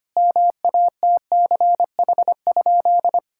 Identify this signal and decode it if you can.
Morse code, heard as a keyed tone, about 25 wpm, 700 Hz. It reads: MATCH?
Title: Sentences from Top 1600 Words 25wpm